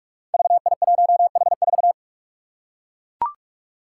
Morse code, heard as a keyed tone, about 42 wpm, 700 Hz. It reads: VI1H4
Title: Five Challenging Letter-Number Combinations 45wpm